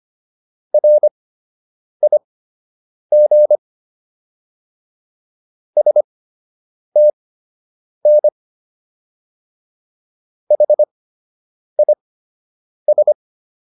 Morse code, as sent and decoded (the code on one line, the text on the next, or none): .-. .. --. / ... - -. / .... .. ...
RIG STN HIS